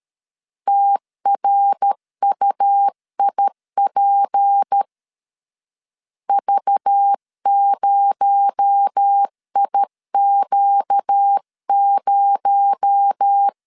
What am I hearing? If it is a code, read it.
Morse code, heard as a keyed tone, about 13 wpm, 800 Hz. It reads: TRUIP V0IQ0
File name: CW / DET / 1 6 1500C S C 12wpm 3533